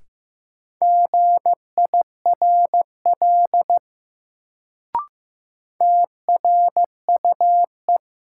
Morse code, heard as a keyed tone, about 15 wpm, 700 Hz. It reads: GIRL TRUE
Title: Sentences from Top 400 Words - New Word Drill - 15wpm